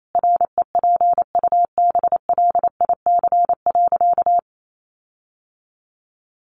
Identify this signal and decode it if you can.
Morse code, heard as a keyed tone, about 28 wpm, 700 Hz. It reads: REPUBLIC.
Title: Sentences from Top 1500 Words 28wpm